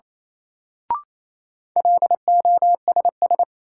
Morse code, heard as a keyed tone, about 28 wpm, 700 Hz. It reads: LOSS
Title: Sentences from Top 1400 Words - New Word Drill Alt - 28wpm